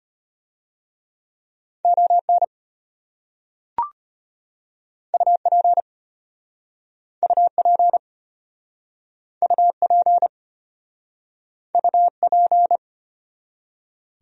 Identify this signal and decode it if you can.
Morse code, heard as a keyed tone, about 34 wpm, 700 Hz. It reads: ON UP UP UP UP